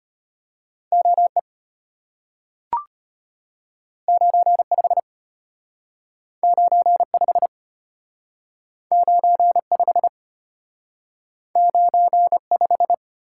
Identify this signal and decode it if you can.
Morse code, heard as a keyed tone, about 34 wpm, 700 Hz. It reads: OE 95 95 95 95